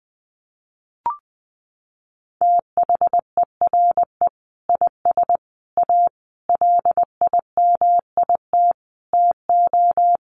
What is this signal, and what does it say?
Morse code, heard as a keyed tone, about 20 wpm, 700 Hz. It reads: THERE IS A LIMIT TO
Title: Sentences from Top 1500 Words 20wpm